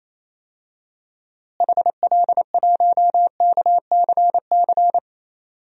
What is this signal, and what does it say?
Morse code, heard as a keyed tone, about 28 wpm, 700 Hz. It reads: HL1KCC
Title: DXCC-100 Call Signs - 6 Characters - Repeated 3x - 28wpm